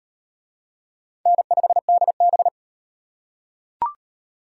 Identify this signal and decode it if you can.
Morse code, heard as a keyed tone, about 37 wpm, 700 Hz. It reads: N5DB